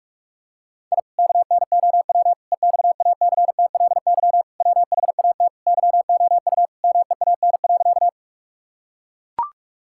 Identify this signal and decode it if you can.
Morse code, heard as a keyed tone, about 41 wpm, 700 Hz. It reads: I KNOW EXACTLY WHAT YOU MEAN.
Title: Sentences from Top 1000 Words - Review Part 6 - 45wpm